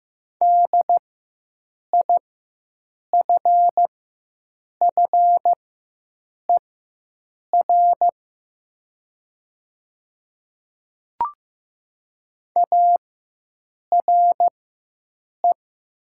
Morse code, heard as a keyed tone, about 15 wpm, spaced as for 7 wpm, 700 Hz. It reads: DIFFER ARE
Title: 4x Character Spacing - Top 200 Words - 15wpm